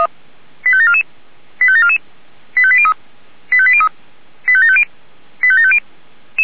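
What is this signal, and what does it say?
Five-tone selective call (ZVEI), heard as quick runs of five tones, tones of about 70 ms. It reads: 85744 85744 85803 85803 85750 85750
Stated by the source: ZVEI five-tone call received by SDR.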